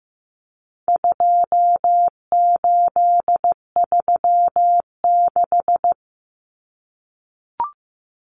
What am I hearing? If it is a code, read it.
Morse code, heard as a keyed tone, about 15 wpm, 700 Hz. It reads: 2836
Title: Common Cut Numbers - 4-Digit 15wpm